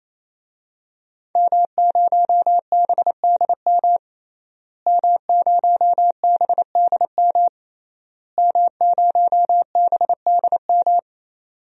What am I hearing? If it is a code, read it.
Morse code, heard as a keyed tone, about 28 wpm, 700 Hz. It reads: M0BDM M0BDM M0BDM
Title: DXCC-100 Call Signs - 5 Characters - Repeated 3x - 28wpm